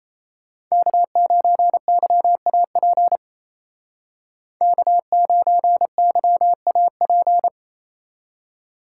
Morse code, heard as a keyed tone, about 36 wpm, 700 Hz. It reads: K9YAP K9YAP